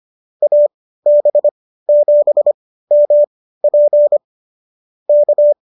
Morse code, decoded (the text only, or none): AB7MP K